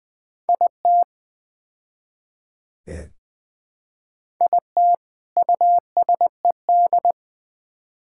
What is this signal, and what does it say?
Morse code, heard as a keyed tone, about 20 wpm, 700 Hz. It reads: IT IT USED
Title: Build A Phrase - 4-Word Phrases - 20wpm